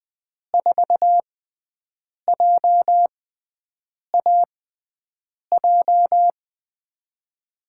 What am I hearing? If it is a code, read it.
Morse code, heard as a keyed tone, about 20 wpm, spaced as for 7 wpm, 700 Hz. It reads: 4JAJ
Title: USA Call Signs - 6x Character Spacing - 20wpm